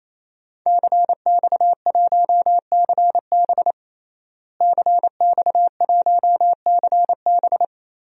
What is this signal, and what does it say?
Morse code, heard as a keyed tone, about 28 wpm, 700 Hz. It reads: CX1CB CX1CB
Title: DXCC-100 Call Signs - 5 Characters - Repeated 3x - 28wpm